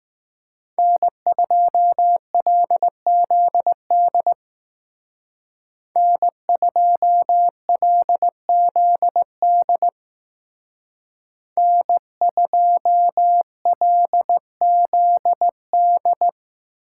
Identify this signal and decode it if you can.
Morse code, heard as a keyed tone, about 20 wpm, 700 Hz. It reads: N2LZD N2LZD N2LZD